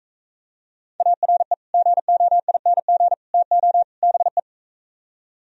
Morse code, heard as a keyed tone, about 45 wpm, 700 Hz. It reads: ARE GOING TO BE